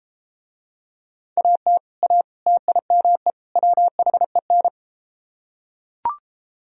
Morse code, heard as a keyed tone, about 33 wpm, 700 Hz. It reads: AT A TIME WHEN